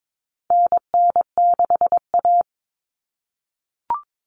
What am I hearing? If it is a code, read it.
Morse code, heard as a keyed tone, about 22 wpm, 700 Hz. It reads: NN6A
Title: Common Cut Numbers - 4-Digit 22wpm